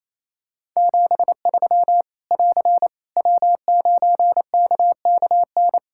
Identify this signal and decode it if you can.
Morse code, heard as a keyed tone, about 27 wpm, 700 Hz. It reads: 73 <AR> W9KKN
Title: K1USN SST QSO Practice 28wpm